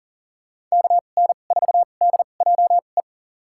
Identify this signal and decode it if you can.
Morse code, heard as a keyed tone, about 40 wpm, spaced as for 29 wpm, 700 Hz. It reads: KN4DJE
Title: USA Call Signs - 2x Character Spacing - 40wpm